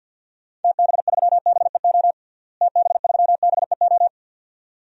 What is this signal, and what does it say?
Morse code, heard as a keyed tone, about 46 wpm, 700 Hz. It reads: TB3BEO TB3BEO
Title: DXCC-100 Call Signs - 6 Characters - Repeated 2x - 50wpm